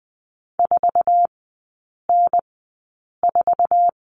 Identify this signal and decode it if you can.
Morse code, heard as a keyed tone, about 20 wpm, spaced as for 8 wpm, 700 Hz. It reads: 4N4
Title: 166 Lesson 27e - Number 7 - Three Characters 20wpm